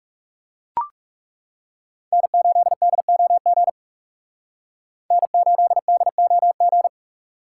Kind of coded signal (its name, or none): Morse code